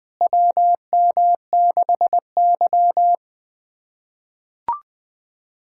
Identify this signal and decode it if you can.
Morse code, heard as a keyed tone, about 20 wpm, 700 Hz. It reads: WM6Y
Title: K1USN SST QSO Practice - 20wpm